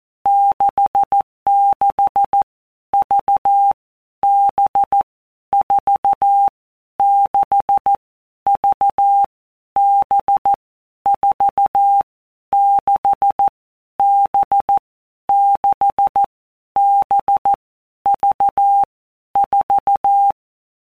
Morse code, heard as a keyed tone, about 14 wpm, 800 Hz. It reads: 66 V B 4 6 V B 4 6 B 6 B V 4
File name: Basic Letter Groups By Blocks / Groups@13 / VB46@13